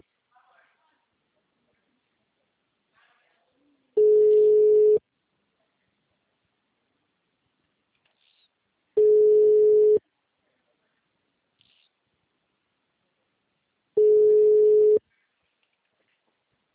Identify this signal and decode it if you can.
Ringback tone (Europe, 425 Hz), with speech.
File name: out- -2007- - - .43946